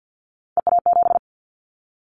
Morse code, heard as a keyed tone, about 50 wpm, 700 Hz. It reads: EA7
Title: Callsign Prefixes for Top 25 DXCC - 50wpm